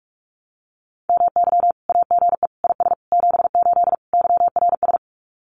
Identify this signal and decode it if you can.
Morse code, heard as a keyed tone, about 44 wpm, 700 Hz. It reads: MY AGE IS 78 YRS
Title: Common QSO Phrases Encore 45wpm